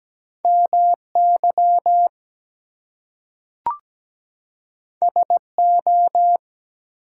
Morse code, heard as a keyed tone, about 17 wpm, 700 Hz. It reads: MY SO
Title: Two Letter Words - 17wpm